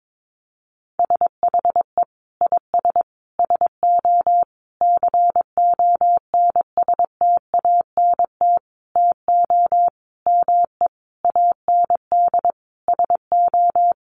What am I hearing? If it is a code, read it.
Morse code, heard as a keyed tone, about 22 wpm, 700 Hz. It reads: SHE IS SO CONSTANT TO ME AND SO
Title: Sentences from Top 1500 Words 22wpm